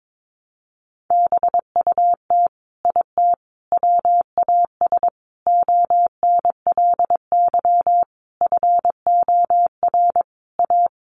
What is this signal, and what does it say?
Morse code, heard as a keyed tone, about 22 wpm, 700 Hz. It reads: BUT IT WAS ONLY FOR A
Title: Sentences from Top 1000 Words 22wpm